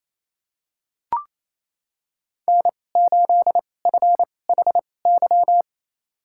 Morse code, heard as a keyed tone, about 28 wpm, spaced as for 20 wpm, 700 Hz. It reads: N8FHY
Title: USA Call Signs - 5 Characters - 2x Character Spacing - 28wpm